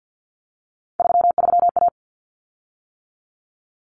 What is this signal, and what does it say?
Morse code, heard as a keyed tone, about 50 wpm, 700 Hz. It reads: 33A
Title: Common Cut Numbers - 3-Digit 50wpm